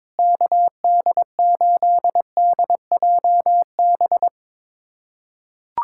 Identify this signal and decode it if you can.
Morse code, heard as a keyed tone, about 22 wpm, 700 Hz. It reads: KD8DJB
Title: USA Callsign DE Callsign 22wpm